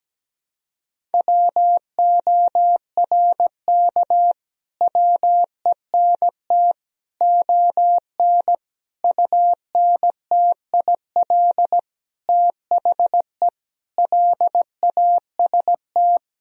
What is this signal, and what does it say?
Morse code, heard as a keyed tone, about 17 wpm, 700 Hz. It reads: WORK WENT ON UNTIL THE LAST
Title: Sentences from Top 1000 Words - Review Part 1 - 17wpm